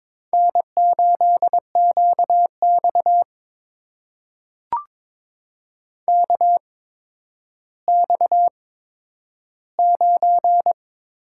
Morse code, heard as a keyed tone, about 22 wpm, spaced as for 11 wpm, 700 Hz. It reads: N8QX KX9
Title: USA Call Signs - 4 Characters - 8x Character Spacing - 22wpm